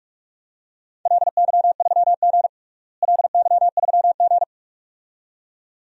Morse code, heard as a keyed tone, about 46 wpm, 700 Hz. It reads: LY3G LY3G
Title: DXCC-100 Call Signs - 4 Characters - Repeated 2x - 45wpm